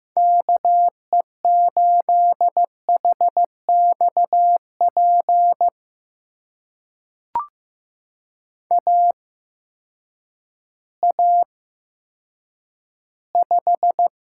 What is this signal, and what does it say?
Morse code, heard as a keyed tone, about 15 wpm, spaced as for 9 wpm, 700 Hz. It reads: KE8HXP AA5
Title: USA Call Signs - 8x Character Spacing - 15wpm